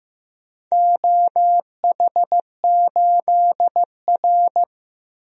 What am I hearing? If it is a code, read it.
Morse code, heard as a keyed tone, about 15 wpm, 700 Hz. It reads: OH8R
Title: DXCC-100 Call Signs - 4 Characters - 15wpm